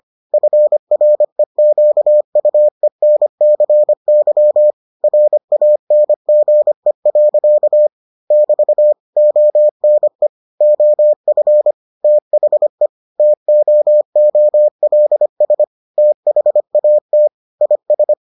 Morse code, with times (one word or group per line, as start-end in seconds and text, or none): 0.3-4.7 FREQUENCY
5.0-7.9 RANGE.
8.3-8.9 <BT>
9.2-10.3 ONE
10.6-11.7 OF
12.0-12.9 THE
13.2-15.6 TOOLS
16.0-17.3 THAT
17.6-18.1 IS